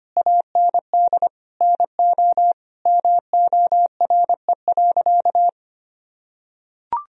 Morse code, heard as a keyed tone, about 25 wpm, 700 Hz. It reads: AND NO MORE.
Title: Sentences from Top 1000 Words - Review Part 2 - 25wpm